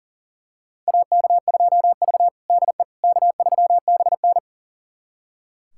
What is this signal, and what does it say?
Morse code, heard as a keyed tone, about 40 wpm, 700 Hz. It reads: AK2V DE K3BN
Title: USA Callsign DE Callsign - 4 Characters - 40wpm